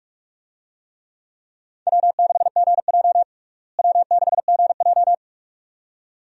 Morse code, heard as a keyed tone, about 41 wpm, 700 Hz. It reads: W6GJ W6GJ
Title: USA Call Signs - 4 Characters - Repeated 2x - 45wpm